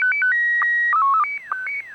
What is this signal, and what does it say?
SSTV (slow-scan television), Scottie 1.
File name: sstv / sstv-pa3eki